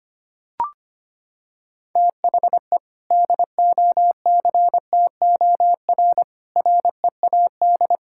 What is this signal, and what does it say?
Morse code, heard as a keyed tone, about 25 wpm, 700 Hz. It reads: THE DOCTOR READ
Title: Sentences from Top 1600 Words 25wpm